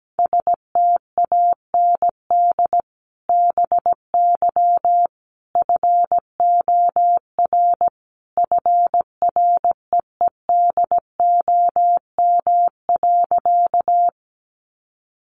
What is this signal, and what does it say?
Morse code, heard as a keyed tone, about 17 wpm, 700 Hz. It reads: STAND BY FOR FREEDOM.
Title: Sentences from Top 800 Words 17wpm